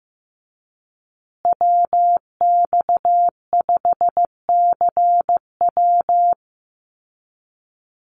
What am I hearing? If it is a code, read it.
Morse code, heard as a keyed tone, about 15 wpm, 700 Hz. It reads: WX5CW
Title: Call Signs Repeated 3x - 15wpm